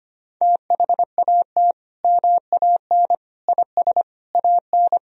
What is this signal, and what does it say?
Morse code, heard as a keyed tone, about 25 wpm, 700 Hz. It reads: THAT MAN IS AN
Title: Sentences from Top 1000 Words - Review Part 1 - 25wpm